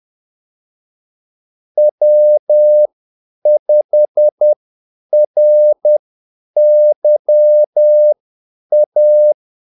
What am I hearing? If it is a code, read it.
Morse code, heard as a keyed tone, about 10 wpm, spaced as for 8 wpm, 600 Hz. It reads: W5RYA
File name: Callsigns-10-8wpm